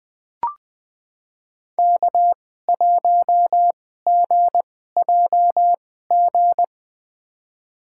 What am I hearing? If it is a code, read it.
Morse code, heard as a keyed tone, about 20 wpm, spaced as for 14 wpm, 700 Hz. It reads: K1GJG
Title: USA Call Signs - 5 Characters - 2x Character Spacing - 20wpm